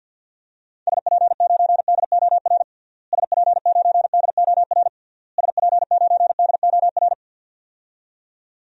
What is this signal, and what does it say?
Morse code, heard as a keyed tone, about 52 wpm, 700 Hz. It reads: SP9DOR SP9DOR SP9DOR
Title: DXCC-100 Call Signs - 6 Characters - Repeated 3x - 50wpm